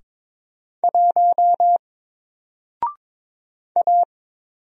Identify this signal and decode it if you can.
Morse code, heard as a keyed tone, about 22 wpm, 700 Hz. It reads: 1 A
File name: Single-Letter-Number-22wpm